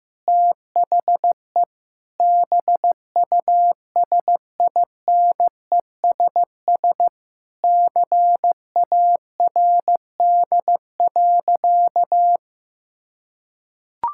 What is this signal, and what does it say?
Morse code, heard as a keyed tone, about 15 wpm, 700 Hz. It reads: THE BUSINESS CARD.
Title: Sentences from Top 1000 Words - Review Part 6 - 15wpm